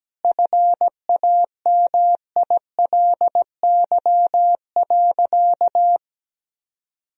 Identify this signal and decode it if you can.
Morse code, heard as a keyed tone, about 17 wpm, 700 Hz. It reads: FAMILY.
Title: Sentences from Top 1000 Words - Review Part 4 - 17wpm